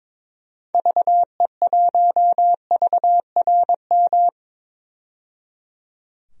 Morse code, heard as a keyed tone, about 21 wpm, 700 Hz. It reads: VE1VRM
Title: DXCC-100 Call Signs - 6 Characters - 22wpm